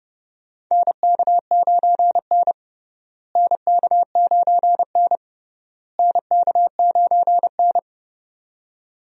Morse code, heard as a keyed tone, about 32 wpm, 700 Hz. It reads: NK9N NK9N NK9N
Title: USA Call Signs - 4 Characters - Repeated 3x - 30wpm